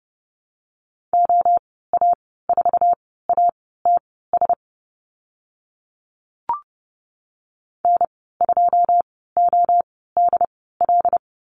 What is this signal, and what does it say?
Morse code, heard as a keyed tone, about 30 wpm, spaced as for 17 wpm, 700 Hz. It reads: OA4ATS N2ODL
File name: Lesson 14g - Number 2 - Call Signs 30wpm